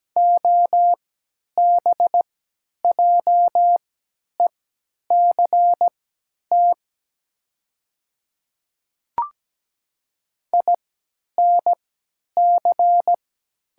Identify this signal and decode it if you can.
Morse code, heard as a keyed tone, about 17 wpm, spaced as for 10 wpm, 700 Hz. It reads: OBJECT INC
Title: 3x Character Spacing - Top 500 Words - 17wpm